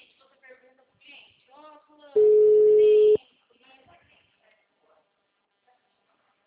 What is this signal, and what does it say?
Ringback tone (Europe, 425 Hz), with speech.